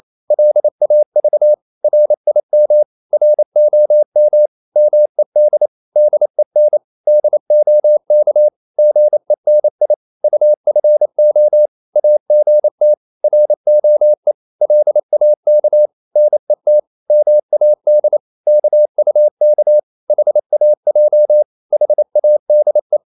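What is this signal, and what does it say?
Morse code, heard as a keyed tone, about 28 wpm, 600 Hz. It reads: LAV RIM ROM MED DEN DOK GENI UFO AGT ROE LAK NET MAD KUK HAJ HADE
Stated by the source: Danish short words sent in Morse at 28 wpm, from CW med Gnister.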